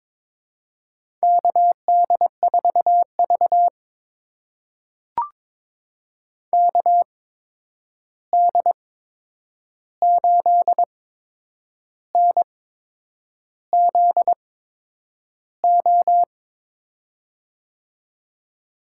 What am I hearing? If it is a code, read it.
Morse code, heard as a keyed tone, about 22 wpm, spaced as for 8 wpm, 700 Hz. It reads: KD4V KD8NZO
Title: USA Call Signs - 8x Character Spacing - 22wpm